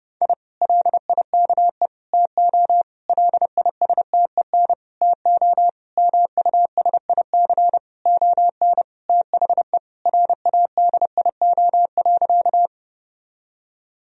Morse code, heard as a keyed tone, about 30 wpm, 700 Hz. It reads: I LIKE TO LISTEN TO MUSIC ON THE RADIO.
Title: Sentences from Top 1000 Words - Review Part 3 - 30wpm